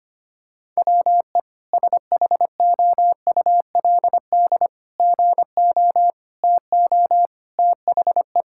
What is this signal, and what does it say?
Morse code, heard as a keyed tone, about 25 wpm, 700 Hz. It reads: WE SHOULD GO TO THE
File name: practice / 25wpm / Sentences with 3-Word Phrases - 25wpm